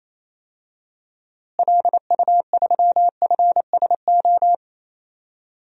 Morse code, heard as a keyed tone, about 28 wpm, 700 Hz. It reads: LU3FSO
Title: DXCC-100 Call Signs - 6 Characters - Repeated 2x - 28wpm